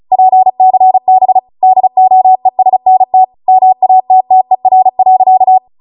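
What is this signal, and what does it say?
Morse code, heard as a keyed tone, about 36 wpm, 750 Hz. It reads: PCB DOESNT MATTER.
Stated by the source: Amateur radio Morse code practice archive transmission at 35 wpm.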